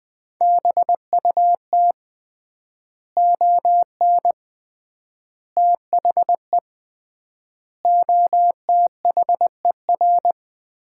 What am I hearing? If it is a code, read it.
Morse code, heard as a keyed tone, about 21 wpm, 700 Hz. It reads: BUT ON THE OTHER
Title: 3x Word-Spacing - Sets of 5 Words - 20wpm